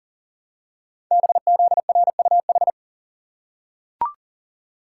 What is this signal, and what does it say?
Morse code, heard as a keyed tone, about 40 wpm, 700 Hz. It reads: BZRUH
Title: Code Groups - 5 Letters - 40wpm